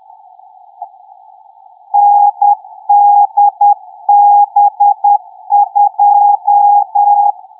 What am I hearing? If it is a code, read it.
Morse code, heard as a keyed tone, about 10 wpm, 820 Hz. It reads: NDB2
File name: pix / NDB2